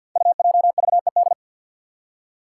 Morse code, heard as a keyed tone, about 53 wpm, 700 Hz. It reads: UJVED
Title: Five Challenging Letter-Number Combinations 50wpm